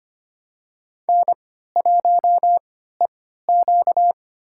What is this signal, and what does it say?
Morse code, heard as a keyed tone, about 25 wpm, spaced as for 14 wpm, 700 Hz. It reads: N1EQ
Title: USA Call Signs - 4 Characters - 3x Character Spacing - 25wpm